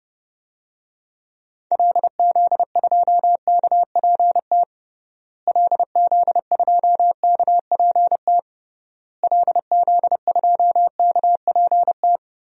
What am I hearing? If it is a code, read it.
Morse code, heard as a keyed tone, about 29 wpm, 700 Hz. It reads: LZ2KPT LZ2KPT LZ2KPT
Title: DXCC-100 Call Signs - 6 Characters - Repeated 3x - 30wpm